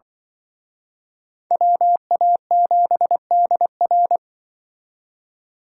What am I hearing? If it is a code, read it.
Morse code, heard as a keyed tone, about 23 wpm, 700 Hz. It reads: WA7DR
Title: Speed Racing - USA 2x2 Call Signs 20wpm